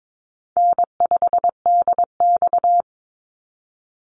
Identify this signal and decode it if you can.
Morse code, heard as a keyed tone, about 22 wpm, 700 Hz. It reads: N5DX
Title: Easy Call Signs - 22wpm